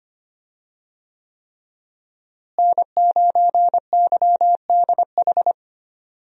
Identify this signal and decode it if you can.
Morse code, heard as a keyed tone, about 25 wpm, 700 Hz. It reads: N9YDH
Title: USA Call Signs - 5 Characters - Repeated 3x - 25wpm